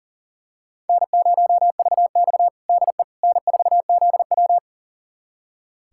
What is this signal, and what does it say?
Morse code, heard as a keyed tone, about 40 wpm, 700 Hz. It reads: N0VX DE N4ZW
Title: USA Callsign DE Callsign - 4 Characters - 40wpm